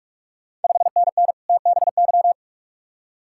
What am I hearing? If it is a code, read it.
Morse code, heard as a keyed tone, about 46 wpm, 700 Hz. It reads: 5NN TBY
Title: Ontario QSO Party Exchange 45wpm